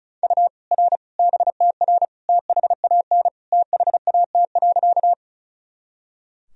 Morse code, heard as a keyed tone, about 36 wpm, 700 Hz. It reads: U R BTR THAN THAT.
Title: Sentences from Top 400 Words and Abbreviations 35wpm